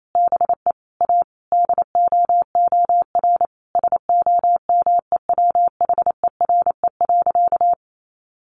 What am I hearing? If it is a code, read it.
Morse code, heard as a keyed tone, about 28 wpm, 700 Hz. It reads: BE A DOOR SOMEWHERE.
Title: Sentences from Top 1400 Words 28wpm